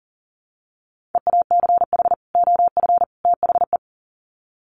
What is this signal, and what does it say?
Morse code, heard as a keyed tone, about 40 wpm, 700 Hz. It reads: EACH OF THE
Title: Sets of 3 Words 40wpm